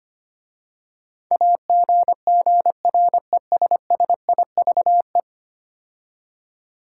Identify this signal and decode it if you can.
Morse code, heard as a keyed tone, about 25 wpm, 700 Hz. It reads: AGGRESSIVE
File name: 10 Letter Words - First 3 Letters Given - 25wpm